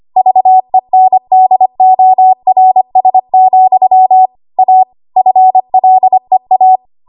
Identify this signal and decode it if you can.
Morse code, heard as a keyed tone, about 25 wpm, 750 Hz. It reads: VENDORS, A FLEA
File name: Morse / 25WPM